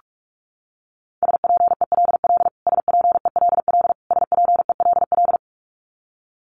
Morse code, heard as a keyed tone, about 44 wpm, 700 Hz. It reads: SPELL SPELL SPELL
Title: Top 500 Words Repeated 45wpm